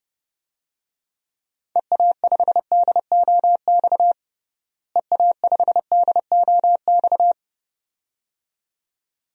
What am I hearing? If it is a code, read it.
Morse code, heard as a keyed tone, about 32 wpm, 700 Hz. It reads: EA5DOX EA5DOX
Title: DXCC-100 Call Signs - 6 Characters - Repeated 2x - 30wpm